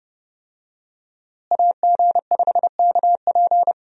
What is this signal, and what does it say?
Morse code, heard as a keyed tone, about 32 wpm, 700 Hz. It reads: AG5KP
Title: USA Call Signs - 5 Characters - Repeated 2x - 30wpm